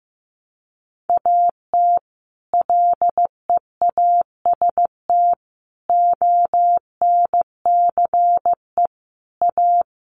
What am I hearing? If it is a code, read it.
Morse code, heard as a keyed tone, about 15 wpm, 700 Hz. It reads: AT LEAST ONCE A
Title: Sets of 5 Words Encore2 15wpm